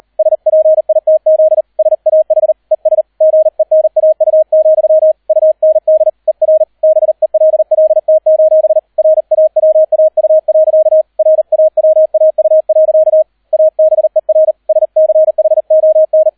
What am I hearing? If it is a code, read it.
Morse code, heard as a keyed tone, about 38 wpm, 630 Hz, with radio noise. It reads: SPITZ SAH ES GENAU, UND ER BELLT: RAWAU. RAWAU. ABER SCHON